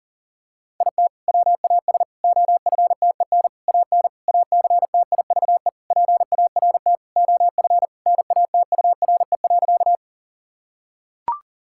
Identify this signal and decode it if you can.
Morse code, heard as a keyed tone, about 40 wpm, 700 Hz. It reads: IT WAS OFTEN AN ACTIVE PART OF NATURE.
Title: Sentences from Top 1000 Words - Review Part 6 - 40wpm